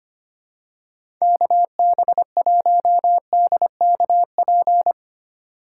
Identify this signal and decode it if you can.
Morse code, heard as a keyed tone, about 25 wpm, 700 Hz. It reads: KB1DKP
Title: USA Call Signs - 6 Characters - 3x Character Spacing - 25wpm